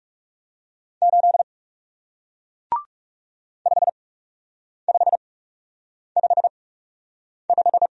Morse code, heard as a keyed tone, about 37 wpm, spaced as for 8 wpm, 700 Hz. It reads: 8 5555
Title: Speed-Racing Single Letter-Number 30wpm